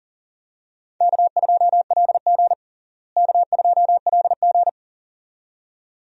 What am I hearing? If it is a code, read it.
Morse code, heard as a keyed tone, about 37 wpm, 700 Hz. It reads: K2LG K2LG
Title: USA Call Signs - 4 Characters - Repeated 2x - 40wpm